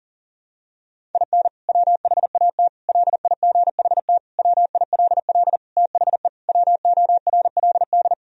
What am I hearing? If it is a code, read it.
Morse code, heard as a keyed tone, about 40 wpm, 700 Hz. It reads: IN WHAT LIGHT WILL THE WORLD
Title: Sentences from Top 1000 Words - Review Part 2 - 40wpm